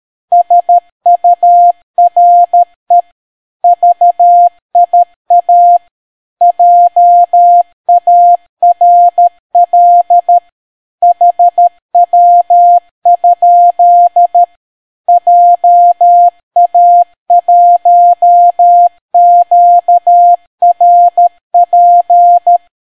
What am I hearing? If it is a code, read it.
Morse code, heard as a keyed tone, about 13 wpm, 700 Hz. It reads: SURE VIA JARL HW? JA1QRP